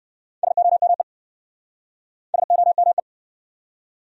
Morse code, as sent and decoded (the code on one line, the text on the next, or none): ... --- -- . / ... --- -- .
SOME SOME